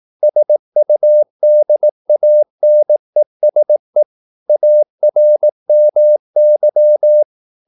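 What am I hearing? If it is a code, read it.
Morse code, heard as a keyed tone, about 18 wpm, 600 Hz. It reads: SUDANESE ARMY
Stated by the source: Morse practice at 18 wpm, English text, from CW med Gnister.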